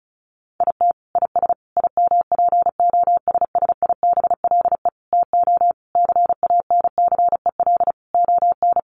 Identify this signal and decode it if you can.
Morse code, heard as a keyed tone, about 34 wpm, 700 Hz. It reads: IT IS IMPOSSIBLE TO CANCEL ON
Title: Sentences from Top 1200 Words 35wpm